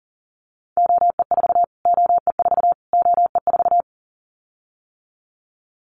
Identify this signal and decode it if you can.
Morse code, heard as a keyed tone, about 40 wpm, 700 Hz. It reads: OE4 OE4 OE4
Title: Callsign Prefixes for Top 25 DXCC - Repeated 3x - 40wpm